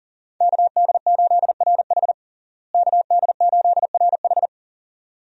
Morse code, heard as a keyed tone, about 37 wpm, 700 Hz. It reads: KD8RH KD8RH
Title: USA Call Signs - 5 Characters - Repeated 2x - 40wpm